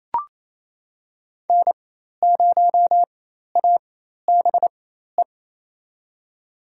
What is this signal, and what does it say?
Morse code, heard as a keyed tone, about 28 wpm, spaced as for 13 wpm, 700 Hz. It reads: N0ABE
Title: USA Call Signs - 4x Character Spacing - 28wpm